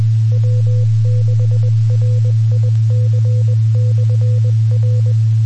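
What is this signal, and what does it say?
Morse code, heard as a keyed tone, about 20 wpm, 500 Hz, with noise. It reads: W6RIC/R
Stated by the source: Morse code recorded on 2 m at 146.850 MHz.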